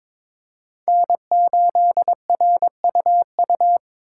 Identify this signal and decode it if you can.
Morse code, heard as a keyed tone, about 22 wpm, 700 Hz. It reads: N8RUU
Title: USA Call Signs - 5 Characters - 8x Character Spacing - 22wpm